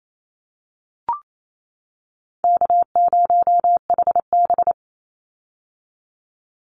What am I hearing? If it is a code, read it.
Morse code, heard as a keyed tone, about 28 wpm, 700 Hz. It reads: K0HB
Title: Call Signs - 28wpm